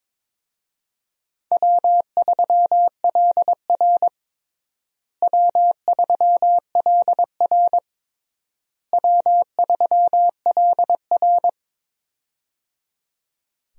Morse code, heard as a keyed tone, about 22 wpm, 700 Hz. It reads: W3LR W3LR W3LR
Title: USA Call Signs - 4 Characters - Repeated 3x - 22wpm